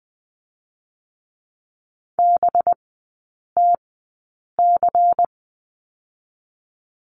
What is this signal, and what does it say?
Morse code, heard as a keyed tone, about 20 wpm, spaced as for 8 wpm, 700 Hz. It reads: BTC
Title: 184 Lesson 30e - Letter b - Three Characters 20wpm